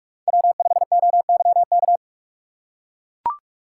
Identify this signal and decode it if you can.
Morse code, heard as a keyed tone, about 41 wpm, 700 Hz. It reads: W5OYK
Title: USA Call Signs - 5 Characters - 3x Character Spacing - 45wpm